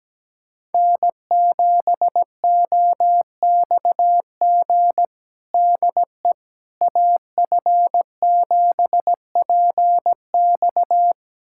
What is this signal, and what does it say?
Morse code, heard as a keyed tone, about 17 wpm, 700 Hz. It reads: N7OXG DE AF7PX
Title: USA Callsign DE Callsign - 5 Characters - 17wpm